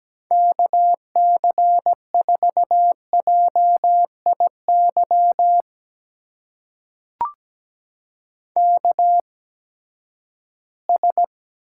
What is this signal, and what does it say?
Morse code, heard as a keyed tone, about 17 wpm, spaced as for 12 wpm, 700 Hz. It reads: KC4JIY KS
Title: USA Call Signs - 8x Character Spacing - 17wpm